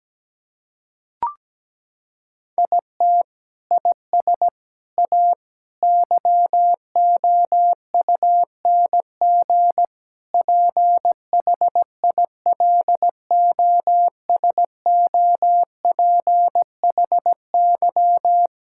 Morse code, heard as a keyed tone, about 17 wpm, 700 Hz. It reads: IT IS A YOUNG PHILOSOPHY
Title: Sentences from Top 1000 Words - Review Part 6 - 17wpm